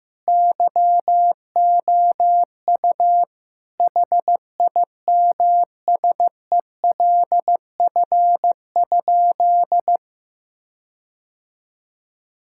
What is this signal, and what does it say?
Morse code, heard as a keyed tone, about 15 wpm, 700 Hz. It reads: YOU HIMSELF?
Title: Sentences from Top 1600 Words 15wpm